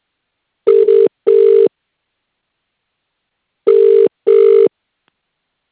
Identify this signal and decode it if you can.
Ringback tone (UK), 400 + 450 Hz.